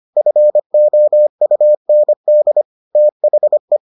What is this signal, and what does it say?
Morse code, heard as a keyed tone, about 25 wpm, 600 Hz. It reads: FOUND THE